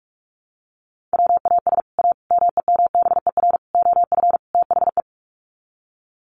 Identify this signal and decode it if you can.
Morse code, heard as a keyed tone, about 44 wpm, 700 Hz. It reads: WAS A MEMBER OF THE
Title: Sets of 5 Words 45wpm